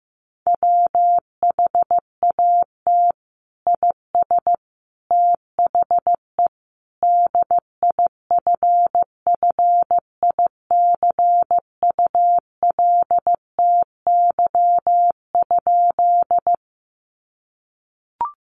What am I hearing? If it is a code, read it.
Morse code, heard as a keyed tone, about 15 wpm, 700 Hz. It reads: WHAT IS THE DIFFICULTY?
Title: Sentences from Top 1500 Words 15wpm